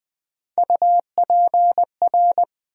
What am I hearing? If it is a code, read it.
Morse code, heard as a keyed tone, about 20 wpm, 700 Hz. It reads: UPR
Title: USA Call Sign Postfixes - 3 Letters 20wpm